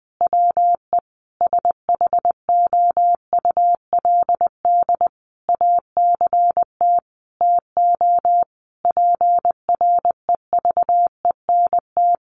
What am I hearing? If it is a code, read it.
Morse code, heard as a keyed tone, about 20 wpm, 700 Hz. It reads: WE SHOULD ACT TO PREVENT
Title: Sentences from Top 1100 Words 20wpm